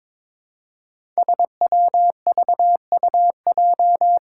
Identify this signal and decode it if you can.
Morse code, heard as a keyed tone, about 22 wpm, 700 Hz. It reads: SWVUJ
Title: Five Challenging Letter-Number Combinations 22wpm